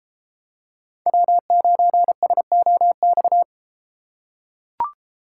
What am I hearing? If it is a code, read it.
Morse code, heard as a keyed tone, about 33 wpm, 700 Hz. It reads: W9SOX